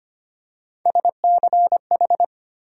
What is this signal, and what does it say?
Morse code, heard as a keyed tone, about 25 wpm, 700 Hz. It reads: SCH
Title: USA Call Sign Postfixes - 3 Letters 25wpm